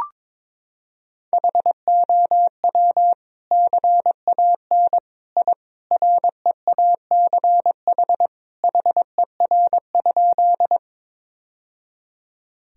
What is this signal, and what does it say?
Morse code, heard as a keyed tone, about 22 wpm, 700 Hz. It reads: HOW CAN I REACH HER?
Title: Sentences from Top 1000 Words - Review Part 2 - 22wpm